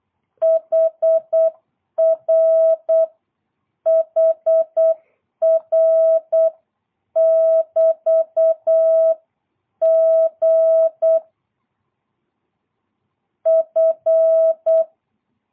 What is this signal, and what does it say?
Morse code, heard as a keyed tone, about 8 wpm, 660 Hz. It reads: HR HR <BT> G F
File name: jh8cbh / 2022cw